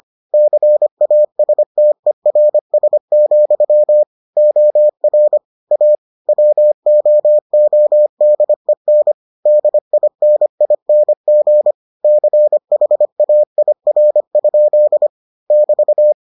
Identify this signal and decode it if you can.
Morse code, heard as a keyed tone, about 25 wpm, 600 Hz. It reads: CASTERS, OR A WOODEN DINING CHAIR? <BT>